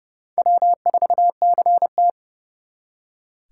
Morse code, heard as a keyed tone, about 38 wpm, 700 Hz. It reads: W4CT